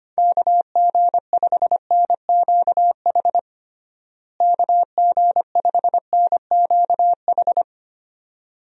Morse code, heard as a keyed tone, about 25 wpm, 700 Hz. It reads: KG5NQH KG5NQH
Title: USA Call Signs - 6 Characters - Repeated 2x - 25wpm